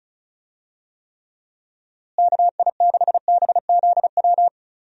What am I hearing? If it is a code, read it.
Morse code, heard as a keyed tone, about 34 wpm, 700 Hz. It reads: KI6BZW
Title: USA Call Signs - 6 Characters - Repeated 3x - 35wpm